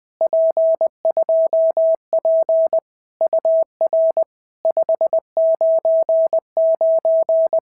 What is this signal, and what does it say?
Morse code, heard as a keyed tone, about 20 wpm, 650 Hz. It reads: P2P UR 599
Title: POTA QSO Practice - 20wpm